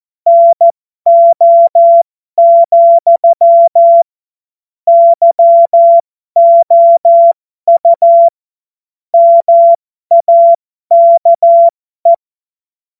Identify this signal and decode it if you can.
Morse code, heard as a keyed tone, about 14 wpm, spaced as for 12 wpm, 680 Hz. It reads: NO, YOU MAKE
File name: Morsekurs / Witze